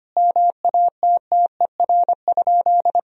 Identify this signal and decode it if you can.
Morse code, heard as a keyed tone, about 25 wpm, 700 Hz. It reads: MATTER?
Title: Sentences from Top 1000 Words - Review Part 3 - 25wpm